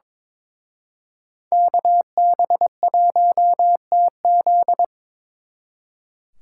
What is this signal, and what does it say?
Morse code, heard as a keyed tone, about 26 wpm, 700 Hz. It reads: KB1TZ
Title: Speed Racing - USA 2x2 Call Signs 22wpm